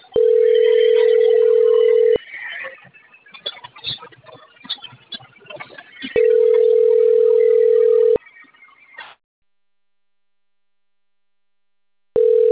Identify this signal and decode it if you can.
Ringback tone (North America), 440 + 480 Hz.
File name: rg-1600-unknown- - - .28132